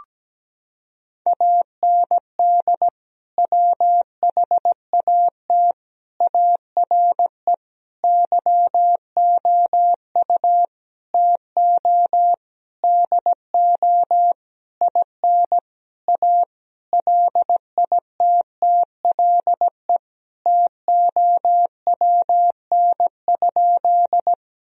Morse code, heard as a keyed tone, about 17 wpm, 700 Hz. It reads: AND WHAT ARE YOU TO DO IN A LITTLE TOWN?
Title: Sentences from Top 1000 Words - Review Part 2 - 17wpm